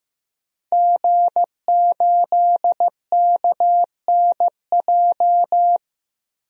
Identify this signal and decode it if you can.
Morse code, heard as a keyed tone, about 15 wpm, 700 Hz. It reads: G8KNJ
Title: DXCC-100 Call Signs - 5 Characters - 15wpm